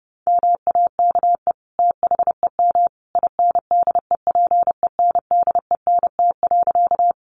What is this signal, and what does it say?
Morse code, heard as a keyed tone, about 30 wpm, 700 Hz. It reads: MAKE THEM INDEPENDENT.
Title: Sentences from Top 900 Words 30wpm